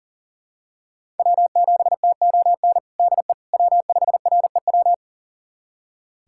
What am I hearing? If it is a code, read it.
Morse code, heard as a keyed tone, about 40 wpm, 700 Hz. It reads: W7TON DE W5REW
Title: USA Callsign DE Callsign - 5 Characters - 40wpm